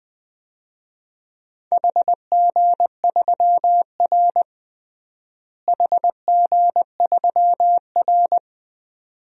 Morse code, heard as a keyed tone, about 20 wpm, 700 Hz. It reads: HG3R HG3R
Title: DXCC-100 Call Signs - 4 Characters - Repeated 2x - 20wpm